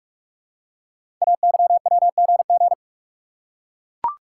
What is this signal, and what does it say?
Morse code, heard as a keyed tone, about 44 wpm, 700 Hz. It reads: AYWGG
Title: Code Groups - 5 Letters - 45wpm